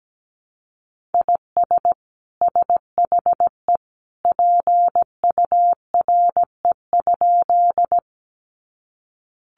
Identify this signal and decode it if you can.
Morse code, heard as a keyed tone, about 17 wpm, 700 Hz. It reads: IS SHE PURE?
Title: Sentences from Top 1500 Words 17wpm